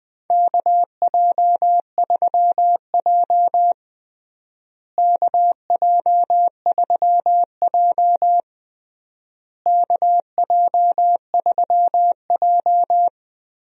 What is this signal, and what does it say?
Morse code, heard as a keyed tone, about 20 wpm, 700 Hz. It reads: KJ3J KJ3J KJ3J
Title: USA Call Signs - 4 Characters - Repeated 3x - 20wpm